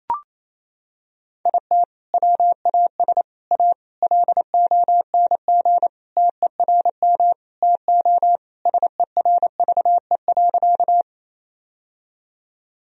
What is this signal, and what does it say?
Morse code, heard as a keyed tone, about 28 wpm, 700 Hz. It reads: IT WAS A LONG TERM TO SERVE.
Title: Sentences from Top 1000 Words - Review Part 4 - 28wpm